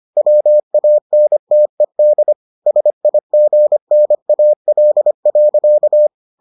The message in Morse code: WANTED SIGNAL.

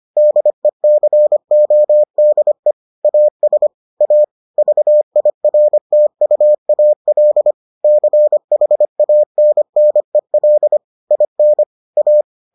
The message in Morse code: DECODE AS A VIRTUAL CHANNEL IN A

